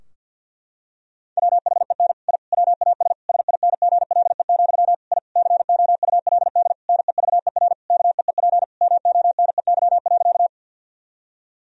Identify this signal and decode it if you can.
Morse code, heard as a keyed tone, about 50 wpm, 700 Hz. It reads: WHEN I WAS SINGLE, I COULD NEVER KEEP MONEY.
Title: Sentences from Top 1000 Words - Review Part 4 - 50wpm